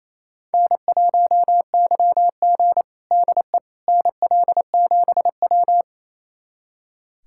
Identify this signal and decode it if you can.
Morse code, heard as a keyed tone, about 28 wpm, 700 Hz. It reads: N1YG DE NL7W
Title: USA Callsign DE Callsign - 4 Characters - 28wpm